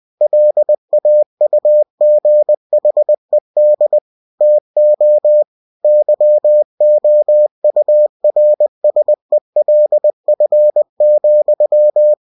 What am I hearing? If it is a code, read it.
Morse code, heard as a keyed tone, about 20 wpm, 600 Hz. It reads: LAUGHED TO YOURSELF,